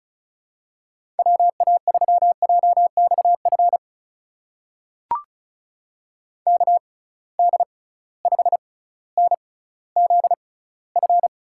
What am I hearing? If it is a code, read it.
Morse code, heard as a keyed tone, about 36 wpm, spaced as for 18 wpm, 700 Hz. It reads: WA3JXF KD5NZF
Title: USA Call Signs - 6 Characters - 6x Character Spacing - 35wpm